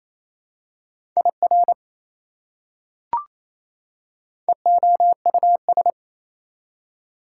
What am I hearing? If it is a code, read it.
Morse code, heard as a keyed tone, about 28 wpm, 700 Hz. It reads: IR EOUS
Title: Prefixes and Suffixes 28wpm